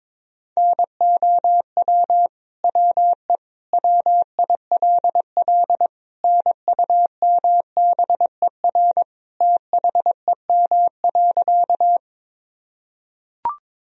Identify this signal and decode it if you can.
Morse code, heard as a keyed tone, about 22 wpm, 700 Hz. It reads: NOW WE WILL NUMBER THEM.
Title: Sentences from Top 1000 Words - Review Part 1 - 22wpm